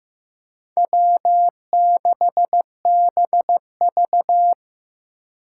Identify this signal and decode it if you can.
Morse code, heard as a keyed tone, about 15 wpm, 700 Hz. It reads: W6BV
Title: Four Challenging Letter-Number Combinations 15wpm